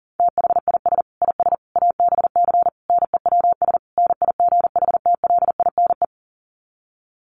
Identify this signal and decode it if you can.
Morse code, heard as a keyed tone, about 40 wpm, 700 Hz. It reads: THIS IS ABC NEWS NIGHTLINE
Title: Sets of 5 Words 40wpm